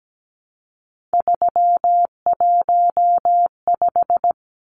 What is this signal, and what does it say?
Morse code, heard as a keyed tone, about 17 wpm, 700 Hz. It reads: 315
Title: Numbers - 3-Digit 17wpm